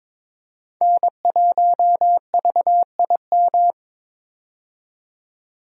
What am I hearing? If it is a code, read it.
Morse code, heard as a keyed tone, about 22 wpm, 700 Hz. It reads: N1VIM
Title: USA Call Signs - 5 Characters - 22wpm